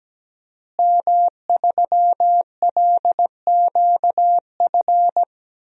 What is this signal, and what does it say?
Morse code, heard as a keyed tone, about 17 wpm, 700 Hz. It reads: M3LQF
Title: DXCC-100 Call Signs - 5 Characters - Repeated 2x - 17wpm